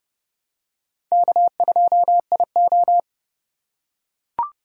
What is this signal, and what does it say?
Morse code, heard as a keyed tone, about 28 wpm, 700 Hz. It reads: K2IO